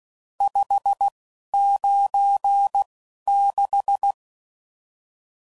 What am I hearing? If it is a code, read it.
Morse code, heard as a keyed tone, about 16 wpm, spaced as for 11 wpm, 800 Hz. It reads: 596
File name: CW Code Groups @10WPM / RSTs / Various RST No's